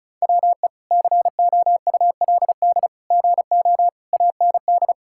Morse code, heard as a keyed tone, about 34 wpm, 700 Hz. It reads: WE COULD GO AND
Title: Sentences from Top 1600 Words 35wpm